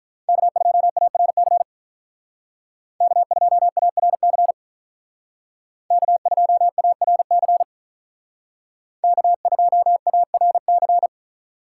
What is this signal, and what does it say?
Morse code, heard as a keyed tone, about 43 wpm, 700 Hz. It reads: K2ARC K2ARC K2ARC K2ARC
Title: Speed Racing - USA Call Signs 35wpm